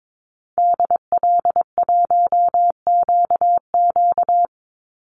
Morse code, heard as a keyed tone, about 22 wpm, 700 Hz. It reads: DL1QQ
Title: Call Signs Repeated 3x - 22wpm